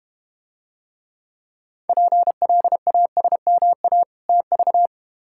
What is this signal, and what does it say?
Morse code, heard as a keyed tone, about 32 wpm, 700 Hz. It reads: PLASMA TV